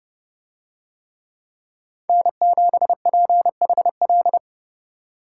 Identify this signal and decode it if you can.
Morse code, heard as a keyed tone, about 29 wpm, 700 Hz. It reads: N7PHL